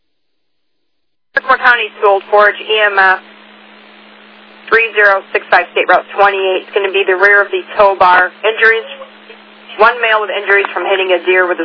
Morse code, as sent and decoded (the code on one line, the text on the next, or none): none